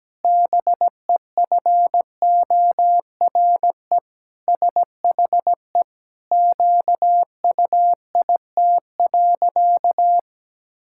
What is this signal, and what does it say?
Morse code, heard as a keyed tone, about 17 wpm, 700 Hz. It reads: BEFORE SHE QUIT.
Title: Pangram Challenge v2 - 17wpm